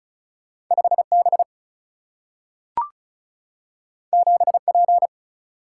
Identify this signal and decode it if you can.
Morse code, heard as a keyed tone, about 36 wpm, 700 Hz. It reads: 5B 7P
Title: Single-Letter Single-Number 35wpm